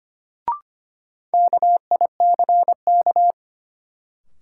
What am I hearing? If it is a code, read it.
Morse code, heard as a keyed tone, about 25 wpm, 700 Hz. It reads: KICK
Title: Sentences from Top 1300 Words - New Word Drill - 25wpm